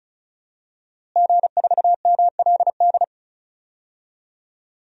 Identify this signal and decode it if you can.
Morse code, heard as a keyed tone, about 34 wpm, 700 Hz. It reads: G4MLD
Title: DXCC-100 Call Signs - 5 Characters - Repeated 3x - 35wpm